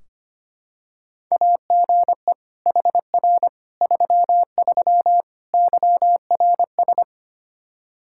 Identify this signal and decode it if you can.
Morse code, heard as a keyed tone, about 25 wpm, 700 Hz. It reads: AGE HR 33 YRS
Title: Practice QSO - Exchange 3 - More Info - 25wpm